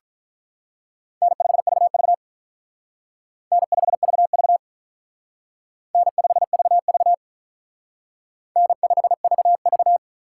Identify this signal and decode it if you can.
Morse code, heard as a keyed tone, about 44 wpm, 700 Hz. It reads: N5VV N5VV N5VV N5VV